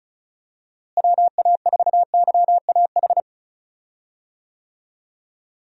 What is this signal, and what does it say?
Morse code, heard as a keyed tone, about 34 wpm, 700 Hz. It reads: WA4YAH
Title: USA Call Signs - 6 Characters - Repeated 2x - 35wpm